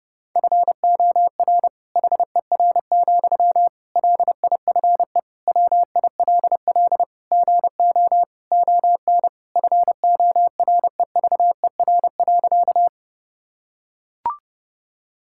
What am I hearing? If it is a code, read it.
Morse code, heard as a keyed tone, about 32 wpm, 700 Hz. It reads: FOR HER, LIFE WILL GO ON FOREVER.
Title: Sentences from Top 1600 Words 30wpm